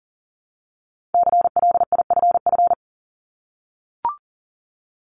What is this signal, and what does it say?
Morse code, heard as a keyed tone, about 42 wpm, 700 Hz. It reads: CLIFF
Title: Multi-letter Phonemes 40wpm